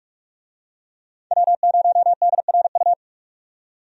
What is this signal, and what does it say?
Morse code, heard as a keyed tone, about 45 wpm, 700 Hz. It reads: W0DRU
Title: USA Call Signs - 5 Characters - 45wpm